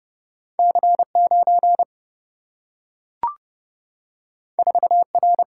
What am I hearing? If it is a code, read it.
Morse code, heard as a keyed tone, about 31 wpm, 700 Hz. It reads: C9 4R
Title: Single-Letter Single-Number 30wpm